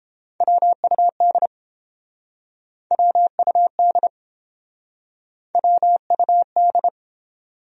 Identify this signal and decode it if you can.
Morse code, heard as a keyed tone, about 31 wpm, 700 Hz. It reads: WUD WUD WUD